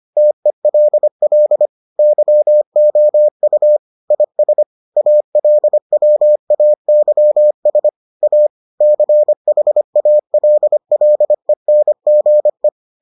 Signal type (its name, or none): Morse code